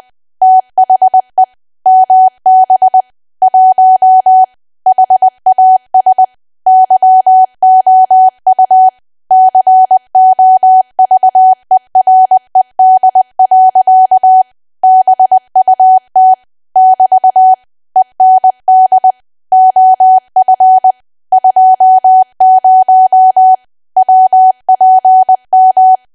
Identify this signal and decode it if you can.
Morse code, heard as a keyed tone, about 20 wpm, 750 Hz. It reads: THE MB 1 HAS YOU COVERED. BUT <BT> END OF 20 WPM
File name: cwa / W1AW-Level3 / 20WPM